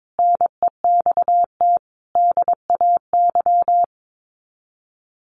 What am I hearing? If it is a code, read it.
Morse code, heard as a keyed tone, about 22 wpm, 700 Hz. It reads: NEXT DAY
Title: Sets of 3 Words 22wpm